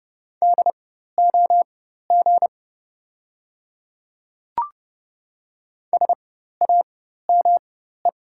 Morse code, heard as a keyed tone, about 30 wpm, spaced as for 14 wpm, 690 Hz. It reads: DOG SAME
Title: 4x Character Spacing - Top 500 Words - 30wpm